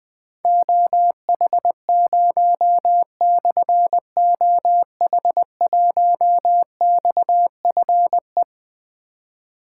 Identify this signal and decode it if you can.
Morse code, heard as a keyed tone, about 20 wpm, 700 Hz. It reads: OH0/OH1XFE
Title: Difficult Call Signs 20wpm